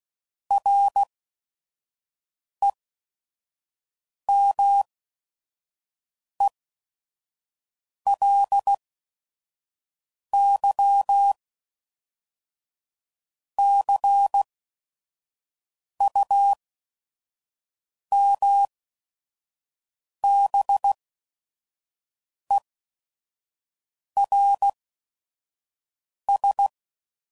Morse code, heard as a keyed tone, about 16 wpm, spaced as for 5 wpm, 800 Hz. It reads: REMELY CUMBERS
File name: CW Code Groups @5WPM / Technical Subjects / AND Gate